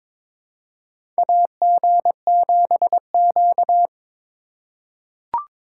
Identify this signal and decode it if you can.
Morse code, heard as a keyed tone, about 22 wpm, 700 Hz. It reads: AG7Q
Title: USA Call Signs - 4 Characters - 2x Character Spacing - 22wpm